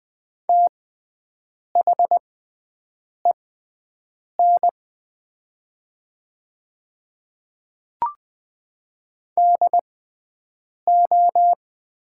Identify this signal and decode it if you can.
Morse code, heard as a keyed tone, about 20 wpm, spaced as for 7 wpm, 700 Hz. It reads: THEN DO
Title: 6x Character Spacing - Top 500 Words - 20wpm